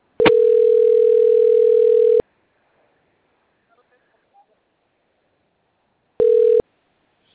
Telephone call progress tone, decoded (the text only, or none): ringback tone